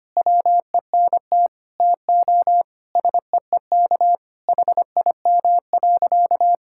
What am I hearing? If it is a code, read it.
Morse code, heard as a keyed tone, about 25 wpm, 700 Hz. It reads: WENT TO SEEK HIM.
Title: Sentences from Top 1600 Words 25wpm